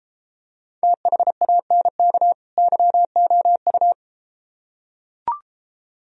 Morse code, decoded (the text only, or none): THANK YOU